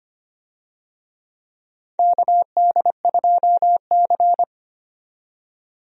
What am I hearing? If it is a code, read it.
Morse code, heard as a keyed tone, about 25 wpm, 700 Hz. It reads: KD2C
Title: USA Call Signs - 4 Characters - 25wpm